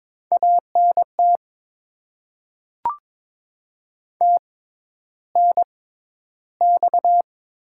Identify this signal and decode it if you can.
Morse code, heard as a keyed tone, about 22 wpm, spaced as for 11 wpm, 690 Hz. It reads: ANT TNX
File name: Top 20 QSO Elements - 6x Character Spacing - 22wpm